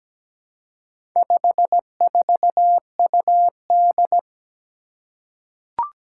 Morse code, heard as a keyed tone, about 17 wpm, 700 Hz. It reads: 54UD
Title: Four Challenging Letter-Number Combinations 17wpm